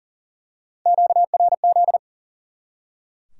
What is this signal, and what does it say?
Morse code, heard as a keyed tone, about 40 wpm, 700 Hz. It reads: QRZ
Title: Top 90 QSO Elements 40wpm